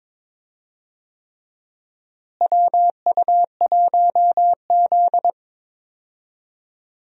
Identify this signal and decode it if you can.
Morse code, heard as a keyed tone, about 22 wpm, 700 Hz. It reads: WU1Z
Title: USA Call Signs - 4 Characters - Repeated 3x - 22wpm